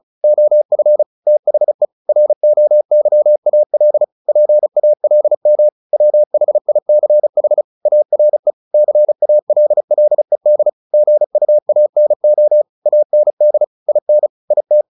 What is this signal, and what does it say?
Morse code, heard as a keyed tone, about 34 wpm, 600 Hz. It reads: OF THE ROYAL PALM WHICH ARE CALLED GUANO AND IN IT
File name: the old man and the sea / the old man and the sea 1 35wpm